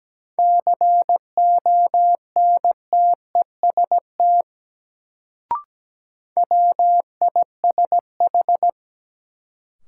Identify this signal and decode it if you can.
Morse code, heard as a keyed tone, about 17 wpm, 700 Hz. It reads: CONTEST WISH
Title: Sentences from Top 1600 Words - New Word Drill Alt - 17wpm